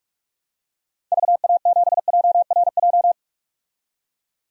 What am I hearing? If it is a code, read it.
Morse code, heard as a keyed tone, about 46 wpm, 700 Hz. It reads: VA7JRJ
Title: DXCC-100 Call Signs - 6 Characters - 45wpm